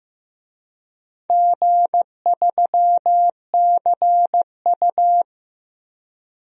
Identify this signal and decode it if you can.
Morse code, heard as a keyed tone, about 16 wpm, 700 Hz. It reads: G3CU
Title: DXCC-100 Call Signs - 4 Characters - Repeated 3x - 15wpm